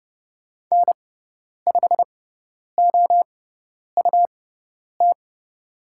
Morse code, heard as a keyed tone, about 30 wpm, spaced as for 10 wpm, 700 Hz. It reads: N5OUT
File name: Lesson 18g - Letter m - Call Signs 30wpm